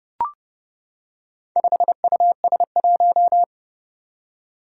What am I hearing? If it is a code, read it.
Morse code, heard as a keyed tone, about 30 wpm, 700 Hz. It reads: 5US1
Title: Four Challenging Letter-Number Combinations 30wpm